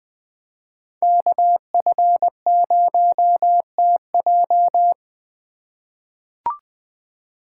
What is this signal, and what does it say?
Morse code, heard as a keyed tone, about 20 wpm, 700 Hz. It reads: KF0TJ
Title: USA Call Signs - 5 Characters - 4x Character Spacing - 20wpm